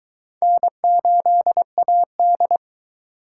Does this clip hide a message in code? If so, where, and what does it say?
Yes: Morse code from 0.4 to 2.6 s, heard as a keyed tone, about 21 wpm, 700 Hz. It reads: N8AD